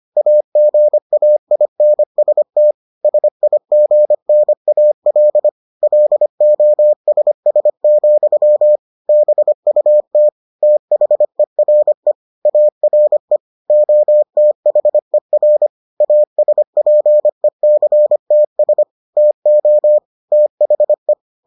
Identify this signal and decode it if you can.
Morse code, heard as a keyed tone, about 25 wpm, 600 Hz. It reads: AGAINST SIGNAL LOSS, BUT THERE ARE OTHER ASPECTS TO THE